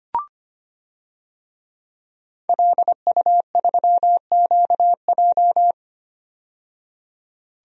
Morse code, heard as a keyed tone, about 25 wpm, 700 Hz. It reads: LU3QJ
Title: DXCC-100 Call Signs - 5 Characters - 25wpm